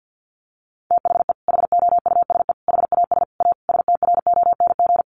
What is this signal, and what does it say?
Morse code, heard as a keyed tone, about 50 wpm, 700 Hz. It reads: THE HOUSE HAS A STRONG
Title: Sentences from Top 1200 Words 50wpm